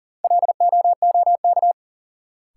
Morse code, heard as a keyed tone, about 37 wpm, 700 Hz. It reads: LOOK
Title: Sentences from Top 100 Words - New Word Drill - 40wpm